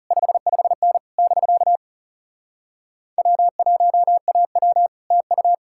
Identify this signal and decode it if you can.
Morse code, heard as a keyed tone, about 40 wpm, 700 Hz. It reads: 55N <BK> W1AW TU